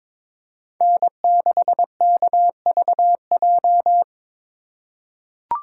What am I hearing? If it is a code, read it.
Morse code, heard as a keyed tone, about 22 wpm, 700 Hz. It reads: N6KVJ
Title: USA Call Signs - 2x Character Spacing - 22wpm